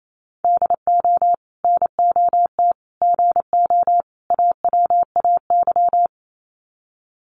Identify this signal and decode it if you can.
Morse code, heard as a keyed tone, about 28 wpm, 700 Hz. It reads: DO NOT GO AWAY
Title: Sets of 4 Words 28wpm